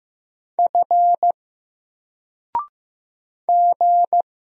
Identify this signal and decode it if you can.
Morse code, heard as a keyed tone, about 15 wpm, 700 Hz. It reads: F G
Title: Single Letter-Number 15wpm